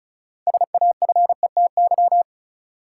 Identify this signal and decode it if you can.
Morse code, heard as a keyed tone, about 36 wpm, 700 Hz. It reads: SAFETY
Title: Sentences from Top 700 Words - New Word Drill - 35wpm